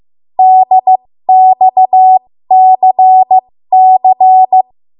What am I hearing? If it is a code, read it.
Morse code, heard as a keyed tone, about 15 wpm, spaced as for 13 wpm, 750 Hz. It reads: DXCC
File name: Morse / 13WPM